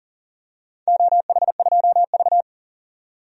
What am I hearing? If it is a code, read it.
Morse code, heard as a keyed tone, about 38 wpm, 700 Hz. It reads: OH2V
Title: DXCC-100 Call Signs - 4 Characters - 40wpm